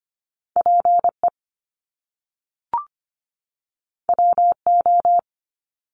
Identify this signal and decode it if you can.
Morse code, heard as a keyed tone, about 25 wpm, 700 Hz. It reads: PE WO
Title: Top Two Letter Combinations in Words 25wpm